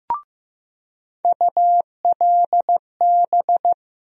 Morse code, heard as a keyed tone, about 15 wpm, 700 Hz. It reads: ULB
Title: USA Call Sign Postfixes - 3 Letters 15wpm